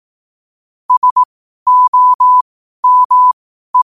Morse code, heard as a keyed tone, about 18 wpm, spaced as for 12 wpm, 1000 Hz. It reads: SOME
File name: qotdfiles / 12wpm